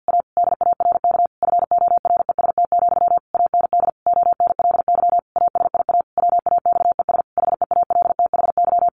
Morse code, heard as a keyed tone, about 50 wpm, 700 Hz. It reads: A DARK FOREST, AND ONLY ASIA WAXES HEALTHY
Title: A Sound Of Thunder 50wpm